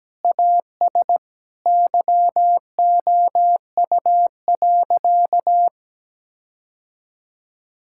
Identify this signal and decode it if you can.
Morse code, heard as a keyed tone, about 17 wpm, 700 Hz. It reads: AS YOU.
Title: Sentences from Top 1600 Words 17wpm